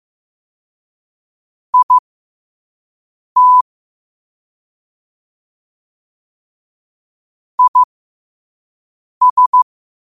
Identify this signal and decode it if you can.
Morse code, heard as a keyed tone, about 15 wpm, spaced as for 5 wpm, 1000 Hz. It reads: IT IS
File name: qotdfiles / 5wpm